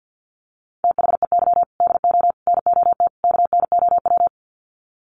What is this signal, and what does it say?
Morse code, heard as a keyed tone, about 48 wpm, 700 Hz. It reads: THEY DO NOT KNOW
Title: Sets of 4 Words 50wpm